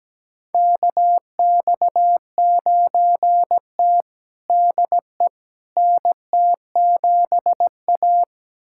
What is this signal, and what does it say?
Morse code, heard as a keyed tone, about 17 wpm, 700 Hz. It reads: KX9T DE NT7A
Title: USA Callsign DE Callsign - 4 Characters - 17wpm